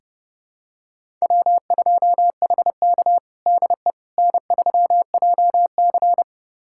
Morse code, heard as a keyed tone, about 30 wpm, 700 Hz. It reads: W2HK DE N3JC
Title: USA Callsign DE Callsign - 4 Characters - 30wpm